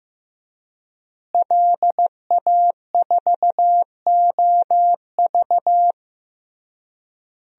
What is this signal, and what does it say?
Morse code, heard as a keyed tone, about 16 wpm, 700 Hz. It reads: LA4OV
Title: DXCC-100 Call Signs - 5 Characters - Repeated 3x - 15wpm